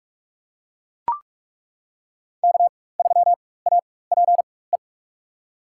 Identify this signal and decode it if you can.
Morse code, heard as a keyed tone, about 41 wpm, spaced as for 20 wpm, 700 Hz. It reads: K3APE
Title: USA Call Signs - 5 Characters - 4x Character Spacing - 45wpm